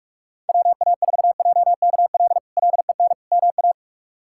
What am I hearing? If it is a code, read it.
Morse code, heard as a keyed tone, about 44 wpm, 700 Hz. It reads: WA4JKL LEN MA
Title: K1USN SST QSO Practice - 45wpm